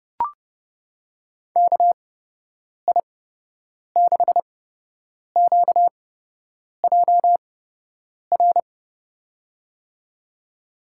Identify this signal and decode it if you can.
Morse code, heard as a keyed tone, about 32 wpm, spaced as for 8 wpm, 700 Hz. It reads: KI6QJR
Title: USA Call Signs - 8x Character Spacing - 30wpm